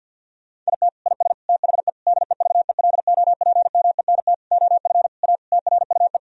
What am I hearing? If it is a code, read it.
Morse code, heard as a keyed tone, about 50 wpm, 700 Hz. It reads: IT IS THE DEVELOPMENT OF A TRUE
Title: Sentences from Top 1000 Words - Review Part 4 - 50wpm